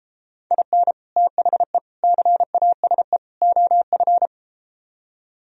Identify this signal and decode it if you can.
Morse code, heard as a keyed tone, about 34 wpm, 700 Hz. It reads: IN THE CASE OF